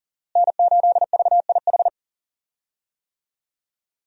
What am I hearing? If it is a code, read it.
Morse code, heard as a keyed tone, about 40 wpm, 700 Hz. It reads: N8VIH
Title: USA Call Signs - 5 Characters - Repeated 2x - 40wpm